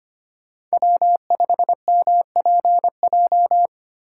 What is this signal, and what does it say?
Morse code, heard as a keyed tone, about 33 wpm, 700 Hz. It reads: W5MPJ